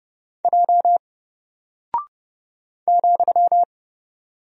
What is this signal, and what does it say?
Morse code, heard as a keyed tone, about 30 wpm, 700 Hz. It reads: J ,
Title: Single Character (Letter:Number:Punctuation:Prosign) 30wpm